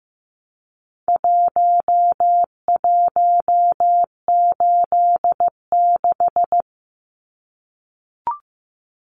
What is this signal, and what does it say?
Morse code, heard as a keyed tone, about 14 wpm, 700 Hz. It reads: 1186
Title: Numbers - Contest Style - 4-Digit 15wpm